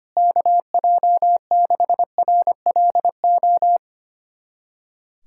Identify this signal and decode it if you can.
Morse code, heard as a keyed tone, about 25 wpm, 700 Hz. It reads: KJ6RLO
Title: USA Callsign DE Callsign 25wpm